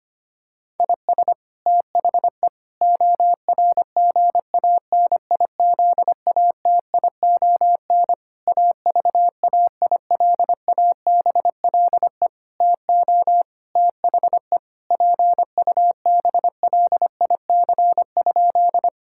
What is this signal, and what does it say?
Morse code, heard as a keyed tone, about 25 wpm, 700 Hz. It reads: IS THE ORGANIZATION AVAILABLE TO THE PUBLIC?
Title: Sentences from Top 1000 Words - Review Part 5 - 25wpm